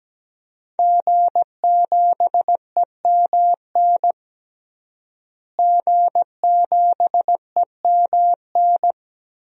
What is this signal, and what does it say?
Morse code, heard as a keyed tone, about 17 wpm, 700 Hz. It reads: G7EMN G7EMN
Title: DXCC-100 Call Signs - 5 Characters - Repeated 3x - 17wpm